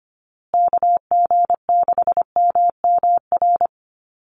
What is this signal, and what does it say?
Morse code, heard as a keyed tone, about 25 wpm, 700 Hz. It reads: KG6MMR
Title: USA Call Signs Repeated 3x 25wpm